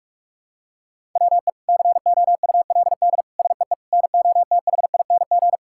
Morse code, heard as a keyed tone, about 41 wpm, 700 Hz. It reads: WE COULD SEE NOTHING
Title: Sentences from Top 1000 Words - Review Part 5 - 45wpm